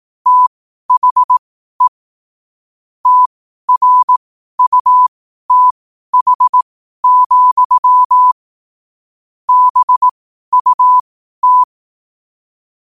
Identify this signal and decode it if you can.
Morse code, heard as a keyed tone, about 18 wpm, spaced as for 12 wpm, 1000 Hz. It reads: THE TRUTH, BUT
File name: qotdfiles / 12wpm